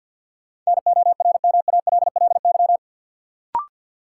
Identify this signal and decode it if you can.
Morse code, heard as a keyed tone, about 46 wpm, 700 Hz. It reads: NORMALLY
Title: Sentences from Top 1000 Words - Review Part 6 - New Word Drill - 50wpm